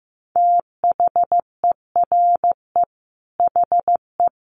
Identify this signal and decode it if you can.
Morse code, heard as a keyed tone, about 14 wpm, 700 Hz. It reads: THERE HE
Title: Sentences from Top 100 Words 15wpm